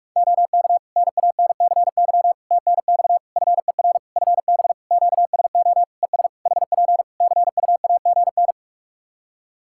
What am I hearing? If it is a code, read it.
Morse code, heard as a keyed tone, about 41 wpm, 700 Hz. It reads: OK NANCY TNX FER FB QSO ES HP CUAGN
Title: Practice QSO - Exchange 4 - Ending - 45wpm